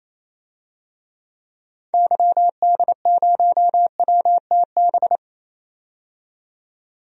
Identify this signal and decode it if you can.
Morse code, heard as a keyed tone, about 28 wpm, 700 Hz. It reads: YD0WTB
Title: DXCC-100 Call Signs - 6 Characters - 28wpm